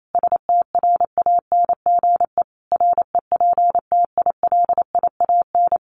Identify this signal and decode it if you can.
Morse code, heard as a keyed tone, about 28 wpm, 700 Hz. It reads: STRANGE REPTILIAN
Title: A Sound Of Thunder 28wpm